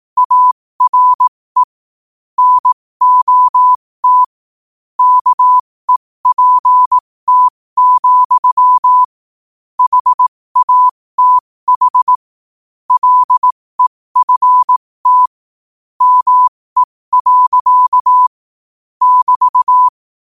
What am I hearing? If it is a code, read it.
Morse code, heard as a keyed tone, about 18 wpm, spaced as for 15 wpm, 1000 Hz. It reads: ARE NOT KEPT, HATH LEFT ME. <BT>